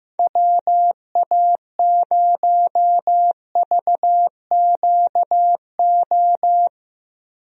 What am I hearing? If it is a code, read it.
Morse code, heard as a keyed tone, about 15 wpm, 700 Hz. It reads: WA0VQO
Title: USA Call Signs - 6 Characters - Repeated 3x - 15wpm